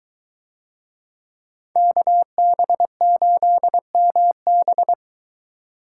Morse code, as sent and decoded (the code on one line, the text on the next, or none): -.- -... ---.. -- -...
KB8MB